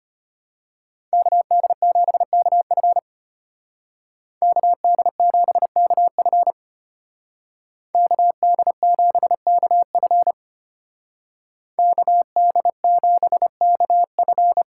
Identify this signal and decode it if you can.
Morse code, heard as a keyed tone, about 30 wpm, 700 Hz. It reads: KD7KF KD7KF KD7KF KD7KF
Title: Speed Racing - USA Call Signs 25wpm